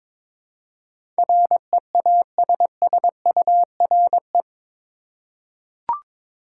Morse code, heard as a keyed tone, about 22 wpm, 700 Hz. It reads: REASSURE
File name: practice / 22wpm / Eight Letter Words - 22wpm